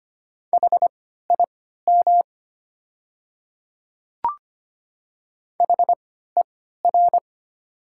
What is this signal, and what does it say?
Morse code, heard as a keyed tone, about 25 wpm, spaced as for 14 wpm, 700 Hz. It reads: HIM HER
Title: 3x Character Spacing - Top 100 Words - 25wpm